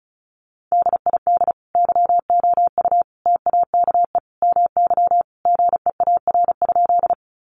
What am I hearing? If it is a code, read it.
Morse code, heard as a keyed tone, about 34 wpm, 700 Hz. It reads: DID YOU TAKE MY GEAR?
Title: Sentences from Top 1400 Words 35wpm